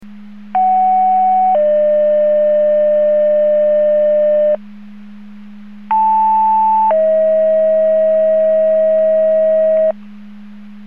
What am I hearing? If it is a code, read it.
Two-tone page, heard as a one-second tone, then a three-second tone, about 750 then 600 Hz, about 880 then 640 Hz. It is heard from 0.5 to 9.9 s.
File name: Portage / 00364 14-32 Portage 154.385 Paging